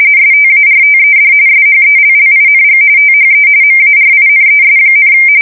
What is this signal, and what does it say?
RTTY, about 45.45 baud. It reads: STARTED SHELLING THE MARINES,
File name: fsk170 / battleshipnj